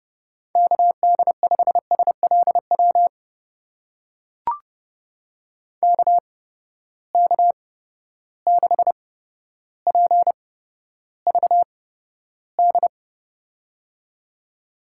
Morse code, heard as a keyed tone, about 30 wpm, spaced as for 13 wpm, 700 Hz. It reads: KD5SLW KK6PVD
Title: USA Call Signs - 6 Characters - 8x Character Spacing - 30wpm